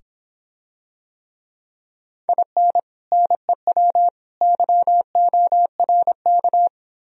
Morse code, heard as a keyed tone, about 26 wpm, 700 Hz. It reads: IN NEW YORK